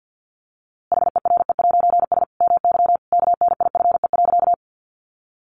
Morse code, heard as a keyed tone, about 50 wpm, 700 Hz. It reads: HERE'S MY KNIFE.
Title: A Sound Of Thunder 50wpm